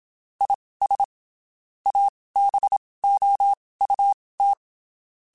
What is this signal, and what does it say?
Morse code, heard as a keyed tone, about 27 wpm, spaced as for 19 wpm, 800 Hz. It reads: IS ABOUT
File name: (spaced) / QSOs Spaced-mp3s / WD0FBT 1 @25 sp